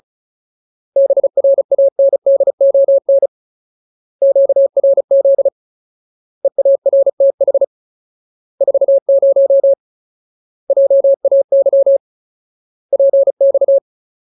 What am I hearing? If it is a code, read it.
Morse code, heard as a keyed tone, about 34 wpm, 550 Hz. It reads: BRANDON QRZ EARTH 40 JAY PX
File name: Ultimate Word List / 35WPM Ultimate